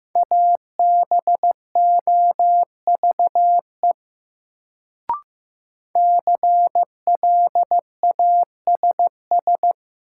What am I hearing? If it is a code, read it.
Morse code, heard as a keyed tone, about 15 wpm, 700 Hz. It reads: ABOVE CLASS
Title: Sentences from Top 1000 Words - Review Part 2 - New Word Drill - 15wpm